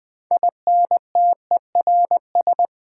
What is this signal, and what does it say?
Morse code, heard as a keyed tone, about 20 wpm, 700 Hz. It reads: INTERS
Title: Build A Word - 6 Letters At A Time - 12-Letter Words - 20wpm